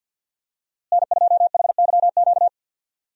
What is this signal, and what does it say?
Morse code, heard as a keyed tone, about 50 wpm, 700 Hz. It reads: N2HYQ
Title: USA Call Signs - 5 Characters - Repeated 2x - 50wpm